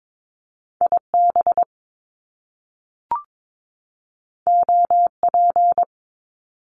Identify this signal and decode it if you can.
Morse code, heard as a keyed tone, about 22 wpm, 700 Hz. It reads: IB OP
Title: Top Two Letter Combinations in Words 22wpm